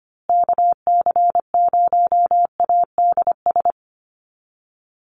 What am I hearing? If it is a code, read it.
Morse code, heard as a keyed tone, about 25 wpm, 700 Hz. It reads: KC0ADS
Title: USA Call Signs Repeated 2x 25wpm